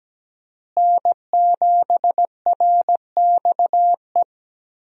Morse code, heard as a keyed tone, about 17 wpm, 700 Hz. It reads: N7RXE
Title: USA Call Signs - 5 Characters - Repeated 2x - 17wpm